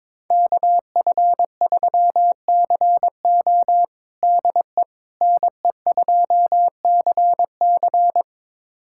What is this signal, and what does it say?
Morse code, heard as a keyed tone, about 22 wpm, 700 Hz. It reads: KF3CO DE NE2CC
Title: USA Callsign DE Callsign 22wpm